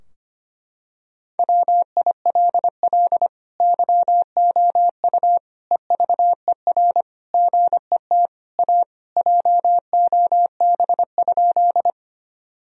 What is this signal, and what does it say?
Morse code, heard as a keyed tone, about 25 wpm, 700 Hz. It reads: WILL YOU EVER GET A JOB?
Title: Sentences from Top 1000 Words - Review Part 3 - 25wpm